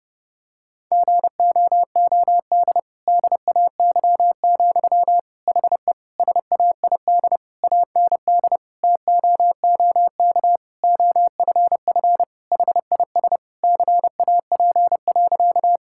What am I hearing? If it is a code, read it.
Morse code, heard as a keyed tone, about 32 wpm, 700 Hz. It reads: GOOD DAY, HE SAID AND TOOK OFF HIS CAP.
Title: Sentences from Top 1600 Words 30wpm